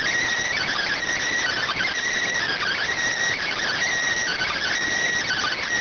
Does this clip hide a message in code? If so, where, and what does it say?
Yes: Emergency Alert System (SAME) tones from 0.6 to 4.7 s, heard as harsh buzzing bursts.